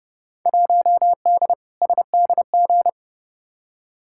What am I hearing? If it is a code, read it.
Morse code, heard as a keyed tone, about 30 wpm, 700 Hz. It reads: 1D SDG
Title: Field Day Exchanges - 30wpm